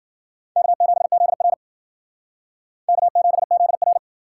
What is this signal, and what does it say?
Morse code, heard as a keyed tone, about 60 wpm, 700 Hz. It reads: K7ZR K7ZR